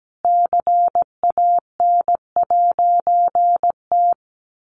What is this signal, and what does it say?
Morse code, heard as a keyed tone, about 17 wpm, 700 Hz. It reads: CAN'T
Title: A Sound Of Thunder 17wpm